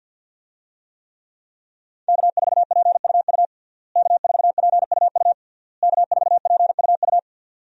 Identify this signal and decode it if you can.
Morse code, heard as a keyed tone, about 50 wpm, 700 Hz. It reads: K4PUU K4PUU K4PUU
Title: USA Call Signs - 5 Characters - Repeated 3x - 50wpm